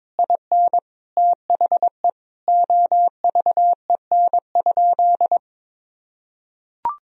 Morse code, heard as a keyed tone, about 22 wpm, 700 Hz. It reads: IN THE OVEN?
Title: Sentences from Top 1000 Words - Review Part 4 - 22wpm